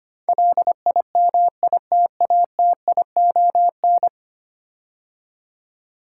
Morse code, heard as a keyed tone, about 25 wpm, 700 Hz. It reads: LIMITATION
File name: 10 Letter Words - First 3 Letters Given - 25wpm